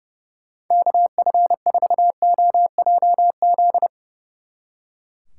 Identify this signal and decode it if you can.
Morse code, heard as a keyed tone, about 40 wpm, 700 Hz. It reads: KF4OJZ